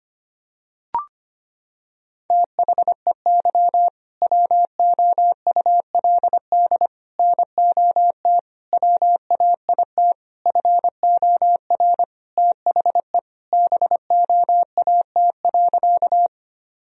Morse code, heard as a keyed tone, about 25 wpm, 700 Hz. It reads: THEY WOULD NOT WAIT FOR THE BOAT.
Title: Sentences from Top 1000 Words - Review Part 3 - 25wpm